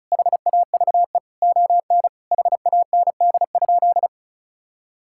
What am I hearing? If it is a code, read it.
Morse code, heard as a keyed tone, about 34 wpm, 700 Hz. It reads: HAVE ON HAND?
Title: Sentences from Top 1000 Words - Review Part 5 - 35wpm